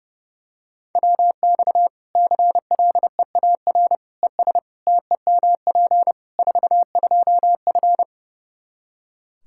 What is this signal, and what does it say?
Morse code, heard as a keyed tone, about 30 wpm, 700 Hz. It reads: WX CLEAR ES TEMP 42F
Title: Practice QSO - Exchange 2 - Info - 30wpm